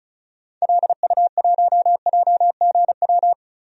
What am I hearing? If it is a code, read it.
Morse code, heard as a keyed tone, about 34 wpm, 700 Hz. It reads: LU1JGW
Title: DXCC-100 Call Signs - 6 Characters - 35wpm